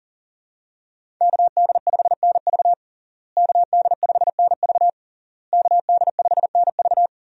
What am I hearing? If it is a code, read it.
Morse code, heard as a keyed tone, about 37 wpm, 700 Hz. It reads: KD5NV KD5NV KD5NV
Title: USA Call Signs - 5 Characters - Repeated 3x - 40wpm